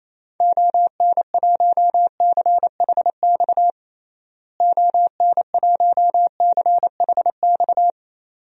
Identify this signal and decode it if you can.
Morse code, heard as a keyed tone, about 28 wpm, 700 Hz. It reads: ON1CHX ON1CHX
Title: DXCC-100 Call Signs - 6 Characters - Repeated 2x - 28wpm